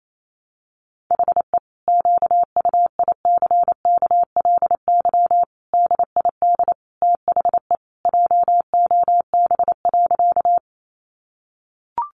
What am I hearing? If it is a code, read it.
Morse code, heard as a keyed tone, about 28 wpm, 700 Hz. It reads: HE QUICKLY DID THE JOB.
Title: Sentences from Top 700 Words 28wpm